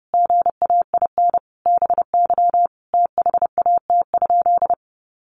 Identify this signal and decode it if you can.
Morse code, heard as a keyed tone, about 30 wpm, 700 Hz. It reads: GAIN BY THAT?
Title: Sentences from Top 1100 Words 30wpm